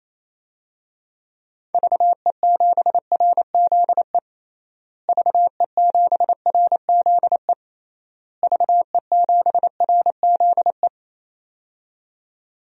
Morse code, heard as a keyed tone, about 28 wpm, 700 Hz. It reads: VE7RZE VE7RZE VE7RZE
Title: DXCC-100 Call Signs - 6 Characters - Repeated 3x - 28wpm